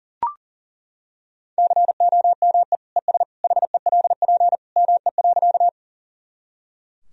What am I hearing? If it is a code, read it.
Morse code, heard as a keyed tone, about 40 wpm, 700 Hz. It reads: COME ES HELP ME.
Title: Sentences from Top 200 Words and Abbreviations 40wpm